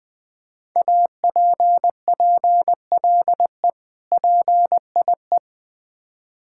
Morse code, heard as a keyed tone, about 20 wpm, 700 Hz. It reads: APPLE PIE
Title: Difficult Words with Hints - 20wpm